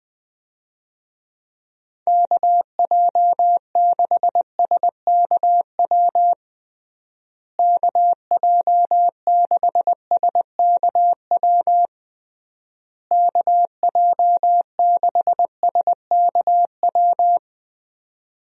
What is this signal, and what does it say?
Morse code, heard as a keyed tone, about 20 wpm, 700 Hz. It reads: KJ6SKW KJ6SKW KJ6SKW
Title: USA Call Signs - 6 Characters - Repeated 3x - 20wpm